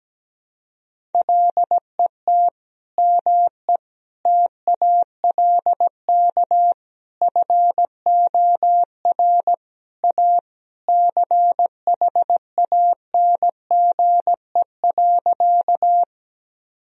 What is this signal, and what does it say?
Morse code, heard as a keyed tone, about 17 wpm, 700 Hz. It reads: LET ME TALK FOR A CHANGE.
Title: Sentences from Top 1000 Words - Review Part 1 - 17wpm